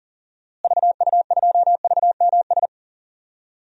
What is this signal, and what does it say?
Morse code, heard as a keyed tone, about 37 wpm, 700 Hz. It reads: VU2VMS
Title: DXCC-100 Call Signs - 6 Characters - Repeated 2x - 40wpm